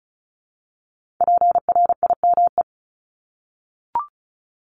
Morse code, heard as a keyed tone, about 36 wpm, 690 Hz. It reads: PRIME
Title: Multi-letter Phonemes 35wpm